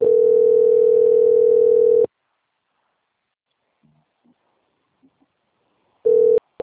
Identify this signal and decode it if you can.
Ringback tone (North America), 440 + 480 Hz.
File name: out- - - - - .38089